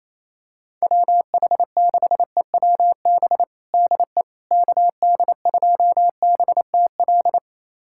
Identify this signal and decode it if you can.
Morse code, heard as a keyed tone, about 28 wpm, 700 Hz. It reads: WH6EWB DE KD2BTL
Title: USA Callsign DE Callsign - 6 Characters - 28wpm